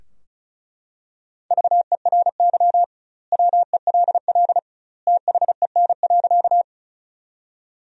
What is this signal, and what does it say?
Morse code, heard as a keyed tone, about 36 wpm, 700 Hz. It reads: VERY WELL THEN.
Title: 2x Word-Spacing - Sentences from Top 200 Words - 35wpm